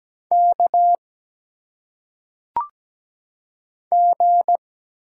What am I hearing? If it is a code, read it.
Morse code, heard as a keyed tone, about 17 wpm, 700 Hz. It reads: K G
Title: Single Letters - Phonetic Pronunciation 17wpm